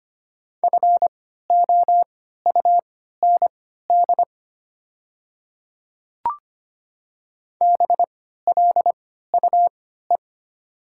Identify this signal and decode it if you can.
Morse code, heard as a keyed tone, about 25 wpm, spaced as for 14 wpm, 700 Hz. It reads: FOUND BLUE
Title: 3x Character Spacing - Top 500 Words - 25wpm